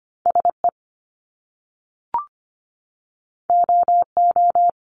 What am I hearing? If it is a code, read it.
Morse code, heard as a keyed tone, about 25 wpm, 700 Hz. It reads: SE OO
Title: Top Two Letter Combinations in Words 25wpm